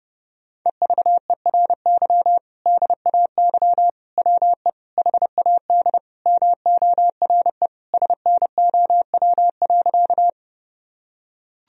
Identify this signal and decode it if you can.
Morse code, heard as a keyed tone, about 30 wpm, 700 Hz. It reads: EVERY DAY WE HAD MORE SNOW.
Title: Sentences from Top 1000 Words - Review Part 3 - 30wpm